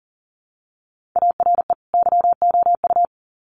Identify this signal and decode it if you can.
Morse code, heard as a keyed tone, about 40 wpm, 700 Hz. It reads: ARE YOU
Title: Sets of 2 Words 40wpm